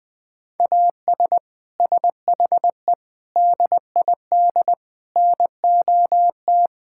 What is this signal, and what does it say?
Morse code, heard as a keyed tone, about 20 wpm, 700 Hz. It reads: AS SHE DID NOT
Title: Sentences from Top 1000 Words - Review Part 5 - 20wpm